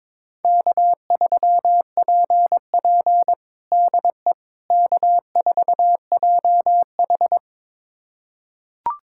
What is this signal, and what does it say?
Morse code, heard as a keyed tone, about 22 wpm, 700 Hz. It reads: K3PP DE K4JH
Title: USA Callsign DE Callsign - 4 Characters - 22wpm